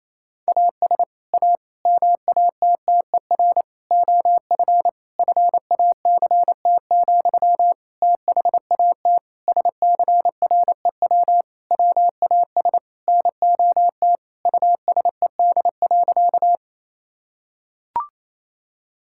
Morse code, heard as a keyed tone, about 28 wpm, 700 Hz. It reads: AS A MATTER OF FACT, THAT SCREW WAS NOT USED.
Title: Sentences from Top 1600 Words 28wpm